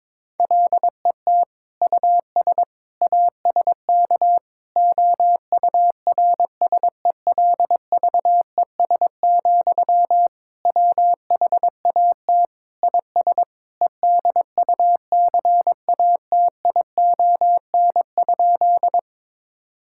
Morse code, heard as a keyed tone, about 22 wpm, 700 Hz. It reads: LET US ASK OURSELVES, WHAT IS EDUCATION?
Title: Sentences from Top 1600 Words 22wpm